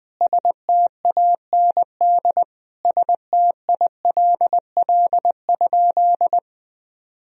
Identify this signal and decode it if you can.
Morse code, heard as a keyed tone, about 21 wpm, 700 Hz. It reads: STAND STILL?
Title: Sentences from Top 1000 Words - Review Part 3 - 20wpm